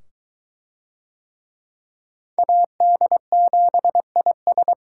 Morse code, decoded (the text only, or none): AD7IS